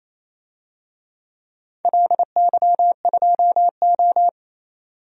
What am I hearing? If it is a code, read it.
Morse code, heard as a keyed tone, about 28 wpm, 700 Hz. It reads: LY2O
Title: DXCC-100 Call Signs - 4 Characters - 28wpm